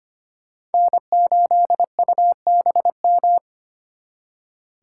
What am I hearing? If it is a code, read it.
Morse code, heard as a keyed tone, about 32 wpm, 700 Hz. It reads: N8UBM